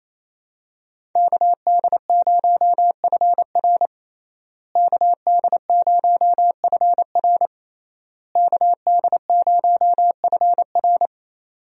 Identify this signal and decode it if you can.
Morse code, heard as a keyed tone, about 28 wpm, 700 Hz. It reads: KD0FR KD0FR KD0FR
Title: USA Call Signs - 5 Characters - Repeated 3x - 28wpm